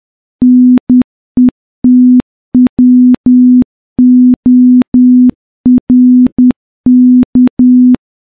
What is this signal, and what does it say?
Morse code, heard as a keyed tone, about 10 wpm, 250 Hz. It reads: NETWORK
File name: morse-training-10WPM